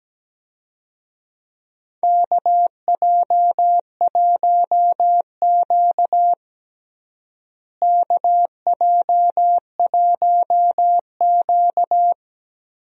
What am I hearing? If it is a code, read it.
Morse code, heard as a keyed tone, about 17 wpm, 700 Hz. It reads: KJ1Q KJ1Q
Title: USA Call Signs - 4 Characters - Repeated 2x - 17wpm